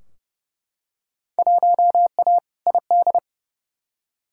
Morse code, heard as a keyed tone, about 30 wpm, 700 Hz. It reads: 1A ID
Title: Field Day Exchanges Encore - 30wpm